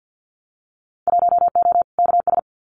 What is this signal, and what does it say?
Morse code, heard as a keyed tone, about 50 wpm, 700 Hz. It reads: 1O KS
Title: Winter Field Day Exchanges 50wpm